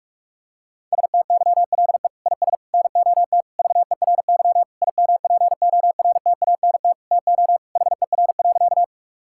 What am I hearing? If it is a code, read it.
Morse code, heard as a keyed tone, about 46 wpm, 700 Hz. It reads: STYLE IS NOT VERY IMPORTANT TO HER.
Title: Sentences from Top 1000 Words - Review Part 5 - 45wpm